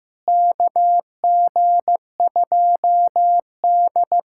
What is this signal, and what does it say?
Morse code, heard as a keyed tone, about 15 wpm, 700 Hz. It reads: KG2D
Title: USA Callsign DE Callsign - 4 Characters - 15wpm